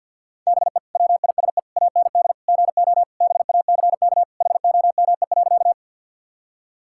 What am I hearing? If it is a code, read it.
Morse code, heard as a keyed tone, about 50 wpm, 700 Hz. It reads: BE WISE AND GO BACK HOME.
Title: Sentences from Top 1600 Words 50wpm